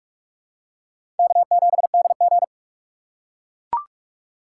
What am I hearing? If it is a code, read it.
Morse code, heard as a keyed tone, about 44 wpm, 700 Hz. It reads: K7DG
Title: USA Call Signs - 3x Character Spacing - 45wpm